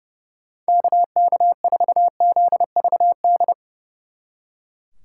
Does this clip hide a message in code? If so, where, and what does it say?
Yes: Morse code from 0.7 to 3.5 s, heard as a keyed tone, about 38 wpm, 700 Hz. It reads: KK4ZVD